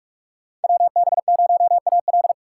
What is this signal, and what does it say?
Morse code, heard as a keyed tone, about 48 wpm, 700 Hz. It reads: WB0AL
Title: USA Call Signs - 5 Characters - 4x Character Spacing - 45wpm